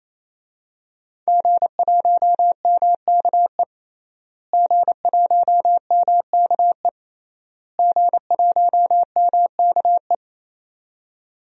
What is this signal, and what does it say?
Morse code, heard as a keyed tone, about 28 wpm, 700 Hz. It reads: G1MKE G1MKE G1MKE
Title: DXCC-100 Call Signs - 5 Characters - Repeated 3x - 28wpm